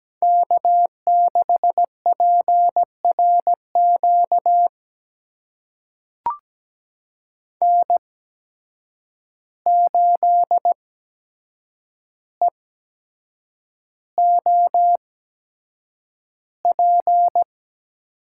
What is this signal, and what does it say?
Morse code, heard as a keyed tone, about 17 wpm, spaced as for 7 wpm, 700 Hz. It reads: K6PRQ N8EOP
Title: USA Call Signs - 5 Characters - 8x Character Spacing - 17wpm